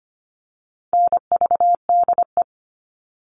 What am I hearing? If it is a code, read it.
Morse code, heard as a keyed tone, about 25 wpm, 700 Hz. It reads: NVDE
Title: All Cut Numbers - 4-Digit 25wpm